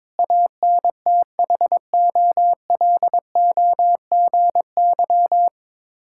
Morse code, heard as a keyed tone, about 22 wpm, 700 Hz. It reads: ANTHOLOGY
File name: Difficult Words with Hints - 22wpm